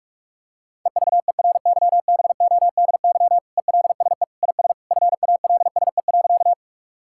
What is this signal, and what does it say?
Morse code, heard as a keyed tone, about 46 wpm, 700 Hz. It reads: EVERYBODY ELSE IS FALSE.
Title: Sentences from Top 1600 Words 45wpm